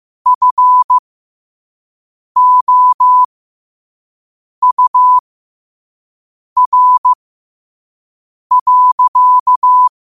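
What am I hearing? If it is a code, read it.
Morse code, heard as a keyed tone, about 15 wpm, spaced as for 5 wpm, 1000 Hz. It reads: FOUR.
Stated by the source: Morse quote-of-the-day practice at 5 wpm.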